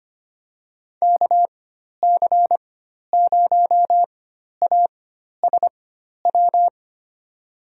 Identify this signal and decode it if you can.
Morse code, heard as a keyed tone, about 25 wpm, spaced as for 12 wpm, 700 Hz. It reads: KC0ASW
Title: USA Call Signs - 6 Characters - 4x Character Spacing - 25wpm